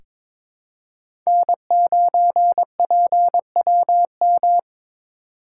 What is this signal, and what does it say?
Morse code, heard as a keyed tone, about 22 wpm, 700 Hz. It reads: N9PWM
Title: USA Call Signs - 5 Characters - 3x Character Spacing - 22wpm